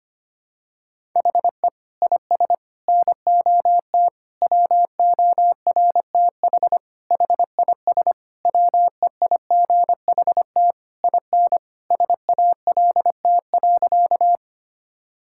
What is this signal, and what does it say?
Morse code, heard as a keyed tone, about 25 wpm, 700 Hz. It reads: HE IS NOT WORTH HIS WEIGHT IN SALT.
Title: Sentences from Top 1000 Words - Review Part 5 - 25wpm